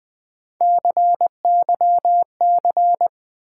Morse code, heard as a keyed tone, about 20 wpm, 700 Hz. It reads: CYC
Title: Build A Word - 1 Letter At A Time - 5-Letter Words - 20wpm